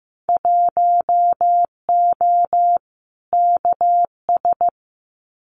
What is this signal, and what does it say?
Morse code, heard as a keyed tone, about 15 wpm, 700 Hz. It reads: 1O KS
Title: Winter Field Day Exchanges 15wpm